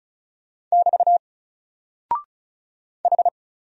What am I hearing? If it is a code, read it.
Morse code, heard as a keyed tone, about 36 wpm, 700 Hz. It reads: <BT> H
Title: Single Character (Letter:Number:Punctuation:Prosign) Alt 35wpm